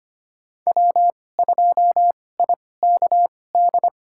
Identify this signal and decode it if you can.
Morse code, heard as a keyed tone, about 25 wpm, spaced as for 18 wpm, 700 Hz. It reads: W2IKD
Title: USA Call Signs - 5 Characters - 2x Character Spacing - 25wpm